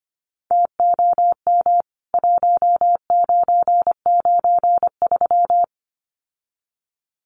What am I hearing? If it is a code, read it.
Morse code, heard as a keyed tone, about 25 wpm, 700 Hz. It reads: TOM 1993
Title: CWT Exchange 25wpm